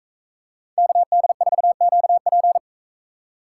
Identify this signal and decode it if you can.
Morse code, heard as a keyed tone, about 44 wpm, 700 Hz. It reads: KD4QP